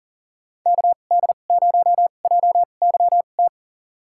Morse code, heard as a keyed tone, about 37 wpm, spaced as for 28 wpm, 700 Hz. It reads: KD0JYT
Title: USA Call Signs - 2x Character Spacing - 40wpm